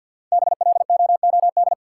Morse code, heard as a keyed tone, about 51 wpm, 700 Hz. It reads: BLOOD
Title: Sentences from Top 1100 Words - New Word Drill Alt - 50wpm